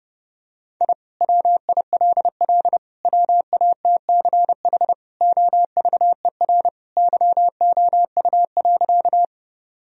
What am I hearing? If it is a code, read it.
Morse code, heard as a keyed tone, about 32 wpm, 700 Hz. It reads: I WILL WATCH OVER YOU.
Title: Sentences from Top 1000 Words - Review Part 1 - 30wpm